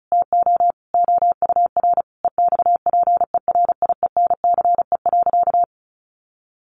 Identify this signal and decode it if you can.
Morse code, heard as a keyed tone, about 34 wpm, 700 Hz. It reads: TO OUR EXPERIENCE.
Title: Sentences from Top 1200 Words 35wpm